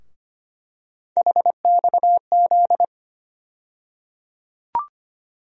Morse code, heard as a keyed tone, about 25 wpm, 700 Hz. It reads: HXZ
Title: USA Call Sign Postfixes - 3 Letters 25wpm